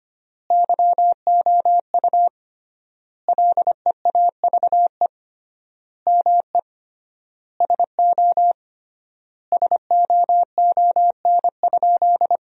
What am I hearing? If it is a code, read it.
Morse code, heard as a keyed tone, about 25 wpm, 700 Hz. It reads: YOU LEAVE ME SO SOON?
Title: 3x Word-Spacing - Sentences from Top 400 Words - 25wpm